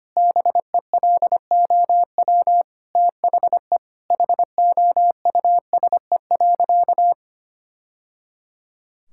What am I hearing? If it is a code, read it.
Morse code, heard as a keyed tone, about 25 wpm, 700 Hz. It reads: BELOW THE HOUSE.
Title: Sentences from Top 1000 Words - Review Part 5 - 25wpm